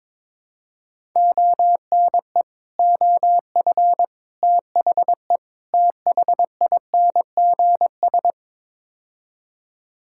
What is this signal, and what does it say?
Morse code, heard as a keyed tone, about 22 wpm, 700 Hz. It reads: ONE OF THE THINGS
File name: practice / 22wpm / Build A Phrase - 5-Word Phrases - 22wpm